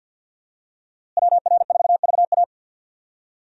Morse code, heard as a keyed tone, about 49 wpm, 700 Hz. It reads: WR4VA
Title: USA Call Signs - 5 Characters - Repeated 2x - 50wpm